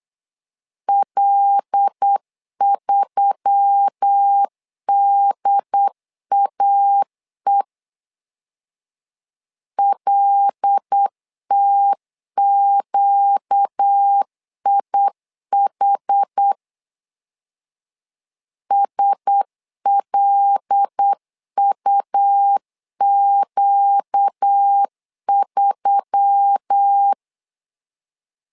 Morse code, heard as a keyed tone, about 8 wpm, 800 Hz. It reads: L3DAE LTQIH SLUQ3
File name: LEZIONI / 5 LBVC / 5 L 8wpm 2831